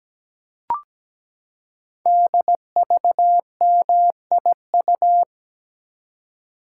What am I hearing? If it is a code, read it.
Morse code, heard as a keyed tone, about 17 wpm, 700 Hz. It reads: DVMIU
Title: Code Groups - 5 Letters - 17wpm